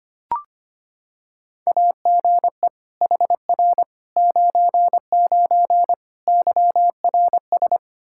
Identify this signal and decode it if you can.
Morse code, heard as a keyed tone, about 25 wpm, 700 Hz. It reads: AGE HR 99 YRS
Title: Practice QSO - Exchange 3 - More Info - 25wpm